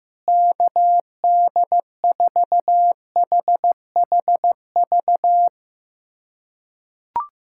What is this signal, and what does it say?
Morse code, heard as a keyed tone, about 16 wpm, 700 Hz. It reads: KD4HHV
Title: USA Callsign DE Callsign 15wpm